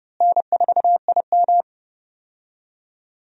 Morse code, heard as a keyed tone, about 30 wpm, 700 Hz. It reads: N4IM
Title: USA Call Signs - 4 Characters - Repeated 2x - 30wpm